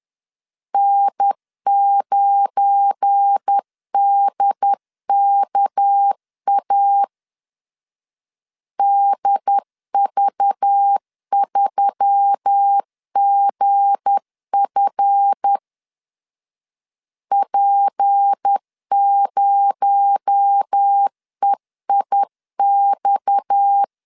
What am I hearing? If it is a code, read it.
Morse code, heard as a keyed tone, about 10 wpm, 800 Hz. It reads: N9DKA DV3GF P0EIX
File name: LEZIONI / 7 GF9Z / 7 GF9 10wpm 2747